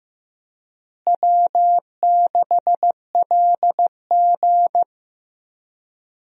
Morse code, heard as a keyed tone, about 15 wpm, 700 Hz. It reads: W6LG
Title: USA Call Signs - 4 Characters - Repeated 3x - 15wpm